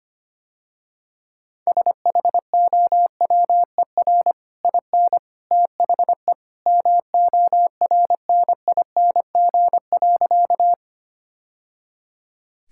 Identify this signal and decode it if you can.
Morse code, heard as a keyed tone, about 25 wpm, 700 Hz. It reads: SHOWER IN THE MORNING.
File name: Easy Sentences with Hints - 25wpm